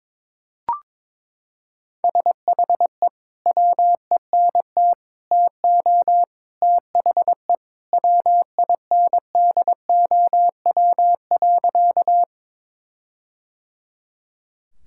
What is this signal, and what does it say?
Morse code, heard as a keyed tone, about 21 wpm, 700 Hz. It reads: SHE WENT TO THE WINDOW.
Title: Sentences from Top 1600 Words 22wpm